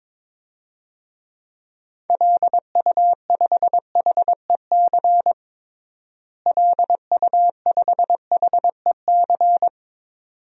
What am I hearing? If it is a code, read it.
Morse code, heard as a keyed tone, about 22 wpm, 700 Hz. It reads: LU5HEC LU5HEC
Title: DXCC-100 Call Signs - 6 Characters - Repeated 2x - 22wpm